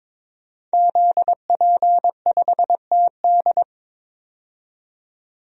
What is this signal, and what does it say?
Morse code, heard as a keyed tone, about 21 wpm, 700 Hz. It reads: ZP5TD
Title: DXCC-100 Call Signs - 5 Characters - 22wpm